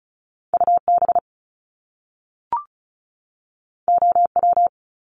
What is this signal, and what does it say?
Morse code, heard as a keyed tone, about 36 wpm, 700 Hz. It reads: UB OW
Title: Top Two Letter Combinations in Words 35wpm